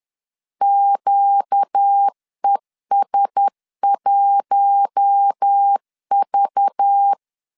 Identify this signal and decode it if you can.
Morse code, heard as a keyed tone, about 10 wpm, 800 Hz. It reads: QES1V
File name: CW / DET / 1 9 1500C S A 10wpm 4511